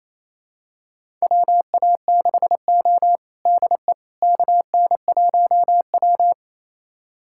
Morse code, heard as a keyed tone, about 28 wpm, 700 Hz. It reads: WA6O DE KN1W
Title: USA Callsign DE Callsign - 4 Characters - 28wpm